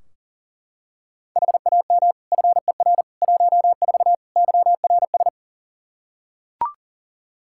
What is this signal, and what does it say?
Morse code, heard as a keyed tone, about 37 wpm, 700 Hz. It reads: HAM FER 14 YRS
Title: K1USN SST QSO Practice 40wpm